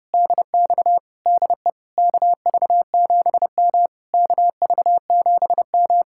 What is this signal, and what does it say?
Morse code, heard as a keyed tone, about 33 wpm, 700 Hz. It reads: DX DE KV7M KV7M